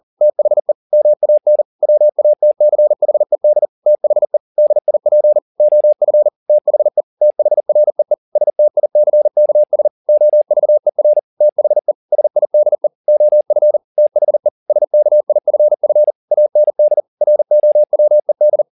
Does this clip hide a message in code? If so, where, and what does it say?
Yes: Morse code from 0.2 to 18.6 s, heard as a keyed tone, about 40 wpm, 600 Hz. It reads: THE MAN WATCHED THE DIP OF THE THREE STICKS OVER THE SIDE OF THE SKIFF AND ROWED